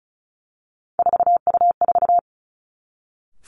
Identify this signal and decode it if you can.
Morse code, heard as a keyed tone, about 36 wpm, 700 Hz. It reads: 4U4
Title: Three Challenging Letter-Number Combinations 35wpm